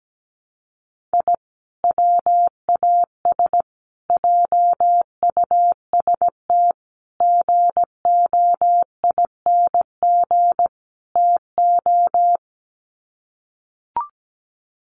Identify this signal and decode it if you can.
Morse code, heard as a keyed tone, about 17 wpm, 700 Hz. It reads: I WAS JUST GOING TO
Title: Sets of 5 Words Encore 17wpm